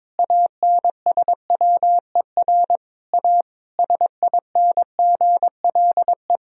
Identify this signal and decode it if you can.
Morse code, heard as a keyed tone, about 22 wpm, 700 Hz. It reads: ANSWER A SINGLE
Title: Sentences from Top 1000 Words - Review Part 4 - 22wpm